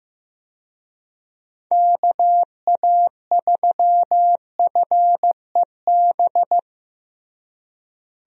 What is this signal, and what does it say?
Morse code, heard as a keyed tone, about 15 wpm, 700 Hz. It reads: KA3FEB
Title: USA Call Signs - 6 Characters - Repeated 3x - 15wpm